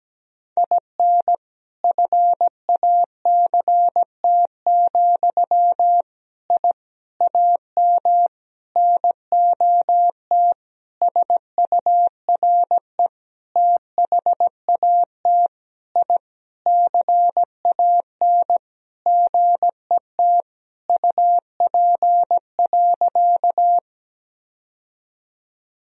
Morse code, heard as a keyed tone, about 17 wpm, 700 Hz. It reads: IN FACT, I AM NOT SURE THAT I CAN GET UP.
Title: Sentences from Top 1000 Words - Review Part 3 - 17wpm